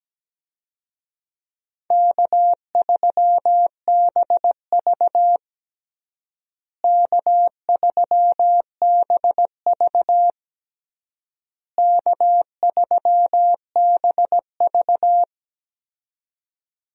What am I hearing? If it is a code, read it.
Morse code, heard as a keyed tone, about 17 wpm, 700 Hz. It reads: K3BV K3BV K3BV
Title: USA Call Signs - 4 Characters - Repeated 3x - 17wpm